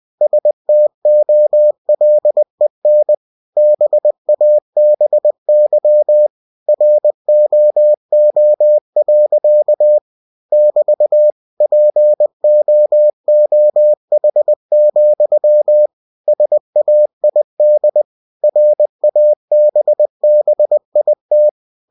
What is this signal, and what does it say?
Morse code, heard as a keyed tone, about 20 wpm, 600 Hz. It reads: STOLEN BABY ROO. <BT> POOH, SAID RABBIT
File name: cw-books / winnie-the-pooh / winnie-20wpm-page-0061